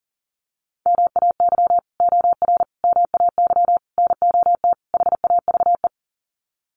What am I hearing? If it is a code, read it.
Morse code, heard as a keyed tone, about 37 wpm, 700 Hz. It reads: MAY OR MAY NOT HAVE
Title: Sets of 5 Words Encore2 40wpm